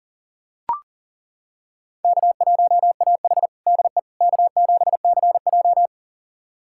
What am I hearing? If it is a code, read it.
Morse code, heard as a keyed tone, about 40 wpm, 700 Hz. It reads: K1AH DE K7CJ
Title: USA Callsign DE Callsign - 4 Characters - 40wpm